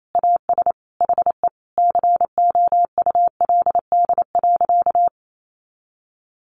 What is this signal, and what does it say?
Morse code, heard as a keyed tone, about 28 wpm, 700 Hz. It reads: AS HE COULD.
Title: Sentences from Top 500 Words 28wpm